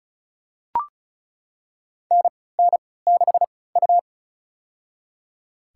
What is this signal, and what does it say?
Morse code, heard as a keyed tone, about 36 wpm, spaced as for 20 wpm, 700 Hz. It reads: NN6U
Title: USA Call Signs - 4 Characters - 3x Character Spacing - 35wpm